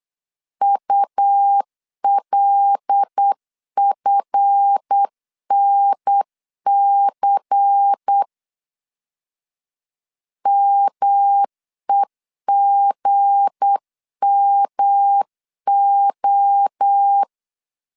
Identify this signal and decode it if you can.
Morse code, heard as a keyed tone, about 8 wpm, 800 Hz. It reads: ULFNC MEGMO
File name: LEZIONI / 7 GF9Z / 7 GF 8wpm 3108